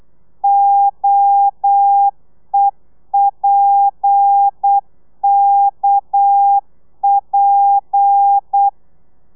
Morse code, heard as a keyed tone, about 8 wpm, 800 Hz. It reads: OEPKP